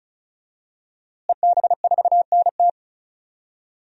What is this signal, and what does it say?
Morse code, heard as a keyed tone, about 36 wpm, 700 Hz. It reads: EB4NT
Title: DXCC-100 Call Signs - 5 Characters - Repeated 2x - 35wpm